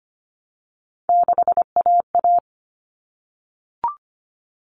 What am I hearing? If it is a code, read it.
Morse code, heard as a keyed tone, about 25 wpm, 700 Hz. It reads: 6AA
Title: All Cut Numbers - 3-Digit 25wpm